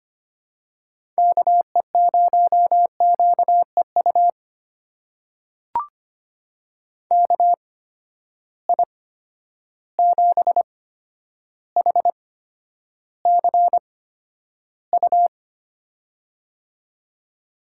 Morse code, heard as a keyed tone, about 25 wpm, spaced as for 11 wpm, 700 Hz. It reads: KE0QEU KI7HCU
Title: USA Call Signs - 6 Characters - 8x Character Spacing - 25wpm